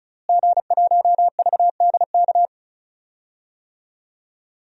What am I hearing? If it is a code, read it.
Morse code, heard as a keyed tone, about 36 wpm, 700 Hz. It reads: G1VDK
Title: DXCC-100 Call Signs - 5 Characters - Repeated 3x - 35wpm